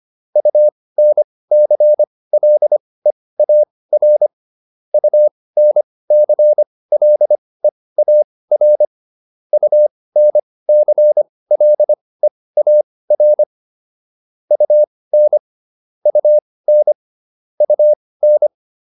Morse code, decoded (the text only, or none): UNCLEAR UNCLEAR UNCLEAR UN UN UN